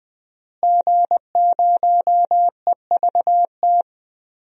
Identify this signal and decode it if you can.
Morse code, heard as a keyed tone, about 20 wpm, 700 Hz. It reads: G0EVT
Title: DXCC-100 Call Signs - 5 Characters - Repeated 3x - 20wpm